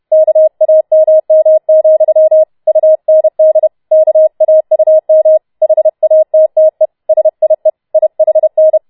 Morse code, heard as a keyed tone, about 31 wpm, 630 Hz. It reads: KAMM, UND KAUM HATTE SIE IHN